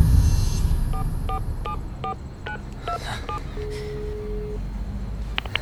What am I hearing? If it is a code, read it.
DTMF, heard as telephone touch tones, tones of about 100 ms. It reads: 4474927